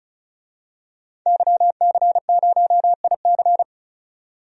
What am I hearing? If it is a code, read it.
Morse code, heard as a keyed tone, about 36 wpm, 700 Hz. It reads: YC0IC
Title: DXCC-100 Call Signs - 5 Characters - 35wpm